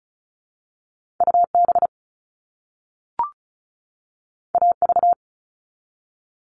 Morse code, heard as a keyed tone, about 36 wpm, 700 Hz. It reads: UB AV
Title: All Cut Numbers - 2-Digit 35wpm